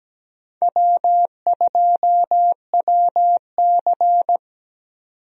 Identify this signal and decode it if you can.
Morse code, heard as a keyed tone, about 17 wpm, 700 Hz. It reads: W2WC
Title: USA Call Signs - 4 Characters - 17wpm